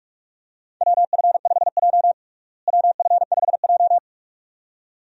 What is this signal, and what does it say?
Morse code, heard as a keyed tone, about 46 wpm, 700 Hz. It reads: WF5J WF5J
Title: USA Call Signs - 4 Characters - Repeated 2x - 45wpm